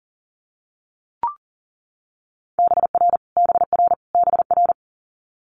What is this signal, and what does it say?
Morse code, heard as a keyed tone, about 39 wpm, 700 Hz. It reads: BR BR BR
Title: Multi-letter Phonemes Repeated 3x 40wpm